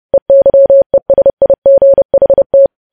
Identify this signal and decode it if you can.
Morse code, heard as a keyed tone, about 30 wpm, 550 Hz. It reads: EYESIGHT
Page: news headlines in morse code at 30 wpm